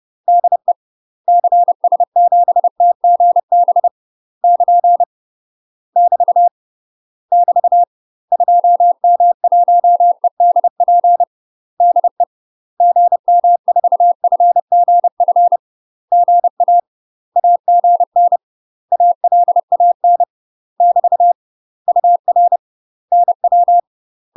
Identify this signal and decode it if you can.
Morse code, heard as a keyed tone, about 30 wpm, 700 Hz. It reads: DE CS7TGB <KN> <BT> <BT> 2M1EDP DE GM4FGF GA AGN ALAN <BT> UR NW
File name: UK MASTER QSOs TRAINING / 30WPM / UK MASTER QSOs 0007